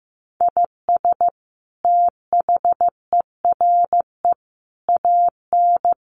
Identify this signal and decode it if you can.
Morse code, heard as a keyed tone, about 15 wpm, 700 Hz. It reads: IS THERE AN
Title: Sentences from Top 1200 Words 15wpm